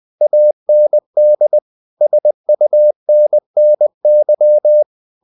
Morse code, heard as a keyed tone, about 20 wpm, 600 Hz. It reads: AND SUNNY
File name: the old man and the sea / the old man and the sea part 1 20wpm